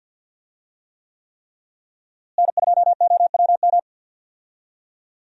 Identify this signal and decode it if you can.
Morse code, heard as a keyed tone, about 51 wpm, 700 Hz. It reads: N2OWM
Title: USA Call Signs - 5 Characters - Repeated 3x - 50wpm